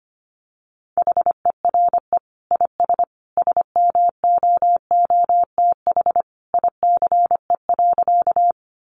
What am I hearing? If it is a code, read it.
Morse code, heard as a keyed tone, about 25 wpm, 700 Hz. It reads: HERE IS SMOOTH ICE.
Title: Sentences from Top 1400 Words 25wpm